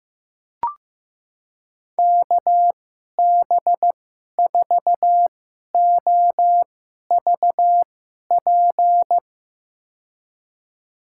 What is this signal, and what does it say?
Morse code, heard as a keyed tone, about 15 wpm, spaced as for 11 wpm, 700 Hz. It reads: KB4OVP
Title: USA Call Signs - 6 Characters - 2x Character Spacing - 15wpm